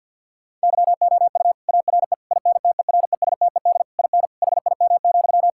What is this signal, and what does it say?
Morse code, heard as a keyed tone, about 50 wpm, 700 Hz. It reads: YOU ARE INTERESTED IN HIM,
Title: Sentences from Top 1000 Words - Review Part 6 - 50wpm